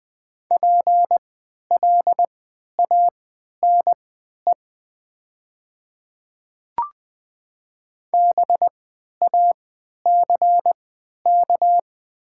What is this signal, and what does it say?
Morse code, heard as a keyed tone, about 20 wpm, spaced as for 11 wpm, 700 Hz. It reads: PLANE BACK
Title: 3x Character Spacing - Top 500 Words - 20wpm